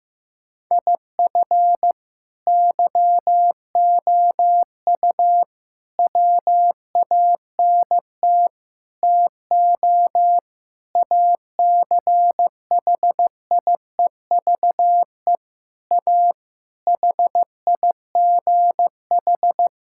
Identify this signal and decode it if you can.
Morse code, heard as a keyed tone, about 15 wpm, 700 Hz. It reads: IF YOU WANT TO ACHIEVE A HIGH
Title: Sentences from Top 1000 Words - Review Part 6 - 15wpm